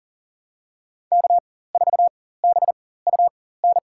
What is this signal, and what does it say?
Morse code, heard as a keyed tone, about 40 wpm, spaced as for 19 wpm, 700 Hz. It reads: K4BUN
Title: USA Call Signs - 5 Characters - 4x Character Spacing - 40wpm